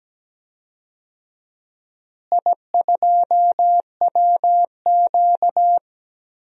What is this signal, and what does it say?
Morse code, heard as a keyed tone, about 17 wpm, 700 Hz. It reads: I2WQ
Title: DXCC-100 Call Signs - 4 Characters - Repeated 3x - 17wpm